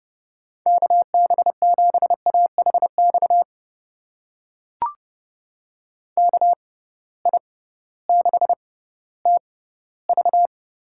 Morse code, heard as a keyed tone, about 30 wpm, spaced as for 16 wpm, 700 Hz. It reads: KB7AHX KI6TV
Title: USA Call Signs - 6 Characters - 6x Character Spacing - 30wpm